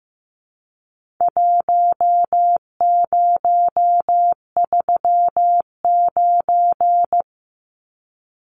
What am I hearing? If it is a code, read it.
Morse code, heard as a keyed tone, about 15 wpm, 700 Hz. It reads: 1039
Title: Numbers - Contest Style - 4-Digit 15wpm